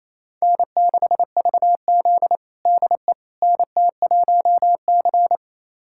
Morse code, heard as a keyed tone, about 27 wpm, 690 Hz. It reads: N6VZ DE NT1C
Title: USA Callsign DE Callsign - 4 Characters - 28wpm